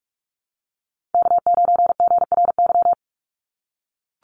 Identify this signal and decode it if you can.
Morse code, heard as a keyed tone, about 44 wpm, 700 Hz. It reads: K9GRY
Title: USA Call Signs 45wpm